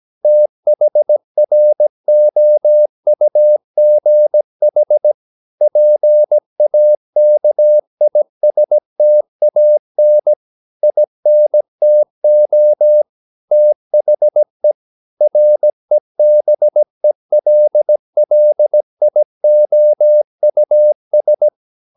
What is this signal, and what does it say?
Morse code, heard as a keyed tone, about 17 wpm, 600 Hz. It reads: THROUGH PAKISTAN INTO THE REBELLIOUS
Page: Romanen Never af Ken Follet på engelsk.